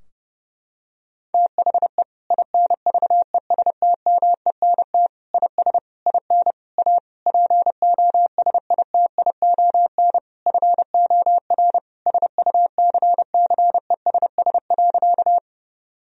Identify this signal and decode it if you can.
Morse code, heard as a keyed tone, about 29 wpm, 700 Hz. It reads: THE INVESTMENT IS IN A POSITION FOR SUCCESS.
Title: Sentences from Top 1000 Words - Review Part 4 - 30wpm